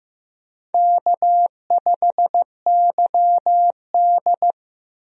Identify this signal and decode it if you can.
Morse code, heard as a keyed tone, about 15 wpm, 700 Hz. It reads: K5YD
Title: USA Call Signs - 4 Characters - Repeated 3x - 15wpm